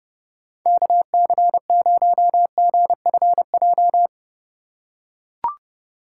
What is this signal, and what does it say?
Morse code, heard as a keyed tone, about 30 wpm, 700 Hz. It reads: KC0GFJ
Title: USA Call Signs - 6 Characters - 2x Character Spacing - 30wpm